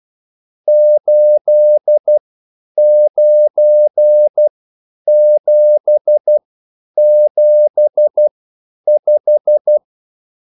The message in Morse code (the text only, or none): 89775